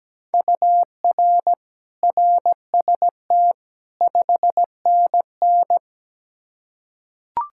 Morse code, heard as a keyed tone, about 17 wpm, 700 Hz. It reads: UR RST 5NN
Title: Practice QSO - Exchange 1 - Beginning - 17wpm